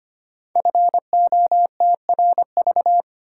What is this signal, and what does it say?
Morse code, heard as a keyed tone, about 25 wpm, 700 Hz. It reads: FOTRV
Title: Code Groups - 5 Letters - 25wpm